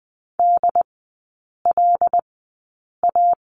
Morse code, heard as a keyed tone, about 20 wpm, spaced as for 8 wpm, 700 Hz. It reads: DLA
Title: 074 Lesson 13e - Letter L - Three Characters 20wpm